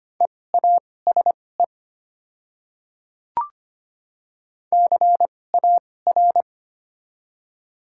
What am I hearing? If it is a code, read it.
Morse code, heard as a keyed tone, about 25 wpm, spaced as for 18 wpm, 700 Hz. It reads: EASE CAR
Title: 2x Character Spacing - Top 300 Words - 25wpm